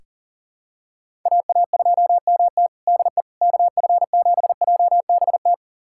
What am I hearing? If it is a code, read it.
Morse code, heard as a keyed tone, about 40 wpm, 700 Hz. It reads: AA2MT DE KF7JBT
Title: USA Callsign DE Callsign 40wpm